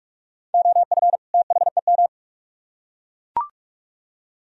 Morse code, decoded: OF THEM

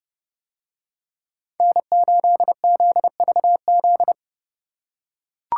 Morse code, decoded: N8ZVZ

